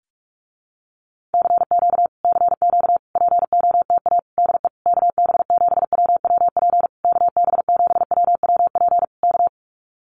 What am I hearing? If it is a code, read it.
Morse code, heard as a keyed tone, about 44 wpm, 700 Hz. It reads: CQ CQ POTA DE KB7WWP KB7WWP K
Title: Common QSO Phrases Encore 45wpm